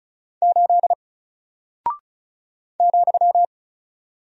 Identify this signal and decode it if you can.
Morse code, heard as a keyed tone, about 36 wpm, 700 Hz. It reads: 8 ,
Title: Single Character (Letter:Number:Punctuation:Prosign) 35wpm